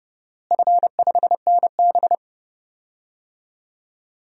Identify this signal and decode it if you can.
Morse code, heard as a keyed tone, about 29 wpm, 700 Hz. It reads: F5NB
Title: DXCC-100 Call Signs - 4 Characters - Repeated 3x - 30wpm